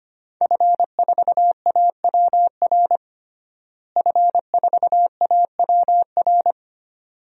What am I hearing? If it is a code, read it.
Morse code, heard as a keyed tone, about 25 wpm, 700 Hz. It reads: F4AWR F4AWR
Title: DXCC-100 Call Signs - 5 Characters - Repeated 2x - 25wpm